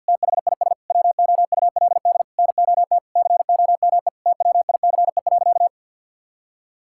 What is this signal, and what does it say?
Morse code, heard as a keyed tone, about 51 wpm, 700 Hz. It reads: THIS WOULD NOT COME TWICE.
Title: Sentences from Top 1600 Words 50wpm